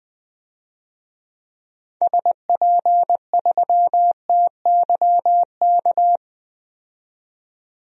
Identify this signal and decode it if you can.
Morse code, heard as a keyed tone, about 20 wpm, 700 Hz. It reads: SP3TYK
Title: DXCC-100 Call Signs - 6 Characters - 20wpm